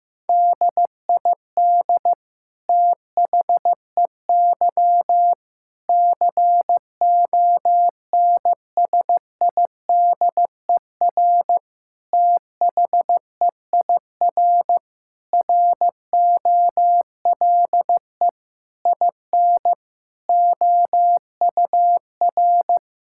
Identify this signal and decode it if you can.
Morse code, heard as a keyed tone, about 15 wpm, 700 Hz. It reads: DID THEY CONSIDER THEIR ROLE IN OUR
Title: Sentences from Top 1000 Words - Review Part 6 - 15wpm